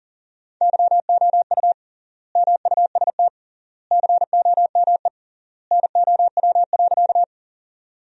Morse code, heard as a keyed tone, about 40 wpm, 700 Hz. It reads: YOU MUST COME NOW.
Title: 3x Word-Spacing - Sentences from Top 200 Words - 40wpm